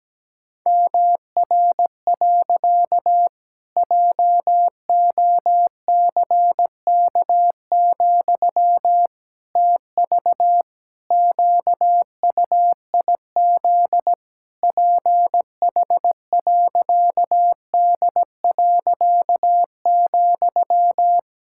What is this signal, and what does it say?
Morse code, heard as a keyed tone, about 17 wpm, 700 Hz. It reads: MR. JOCK, TV QUIZ PH.D.,
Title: Pangram Challenge v2 - 17wpm